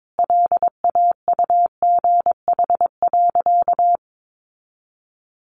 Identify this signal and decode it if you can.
Morse code, heard as a keyed tone, about 22 wpm, 700 Hz. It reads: LAUGH.
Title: Sentences from Top 1100 Words 22wpm